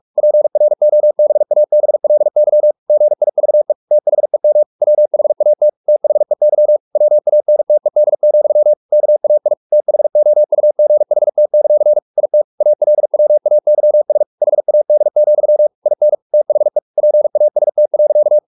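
Morse code, heard as a keyed tone, about 44 wpm, 600 Hz. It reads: PROBABLY GIVE THEM WHAT THEY WANTED, KAI THOUGHT; IT ALWAYS HAD, IN THE PAST.
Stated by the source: Morse practice at 45 wpm, the novel Never by Ken Follett in English, from CW med Gnister.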